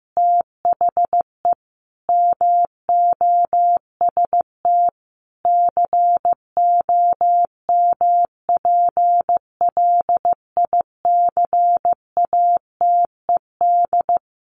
Morse code, heard as a keyed tone, about 15 wpm, 700 Hz. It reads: THE MOST COMPLICATED
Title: Sentences from Top 1200 Words 15wpm